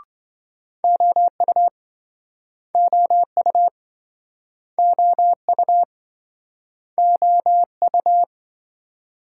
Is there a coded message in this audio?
Yes: Morse code from 0.8 to 8.2 s, heard as a keyed tone, about 28 wpm, 700 Hz. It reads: OU OU OU OU